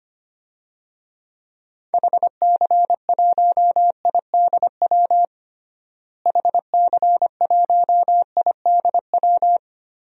Morse code, heard as a keyed tone, about 25 wpm, 700 Hz. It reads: HC1IDW HC1IDW
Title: DXCC-100 Call Signs - 6 Characters - Repeated 3x - 25wpm